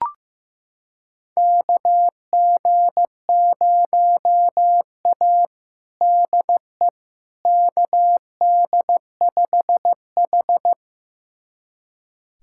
Morse code, heard as a keyed tone, about 16 wpm, 700 Hz. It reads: KG0A DE KD5H
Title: USA Callsign DE Callsign - 4 Characters - 15wpm